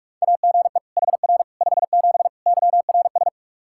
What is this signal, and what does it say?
Morse code, heard as a keyed tone, about 46 wpm, 700 Hz. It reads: AGE HR 57 YRS
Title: Practice QSO - Exchange 3 - More Info - 45wpm